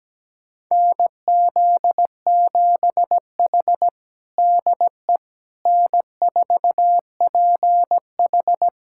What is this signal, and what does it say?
Morse code, heard as a keyed tone, about 17 wpm, 700 Hz. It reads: NZ7H DE N4PH
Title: USA Callsign DE Callsign - 4 Characters - 17wpm